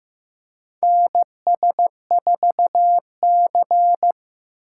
Morse code, heard as a keyed tone, about 15 wpm, 700 Hz. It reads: NS4C
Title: USA Call Signs - 4 Characters - 4x Character Spacing - 15wpm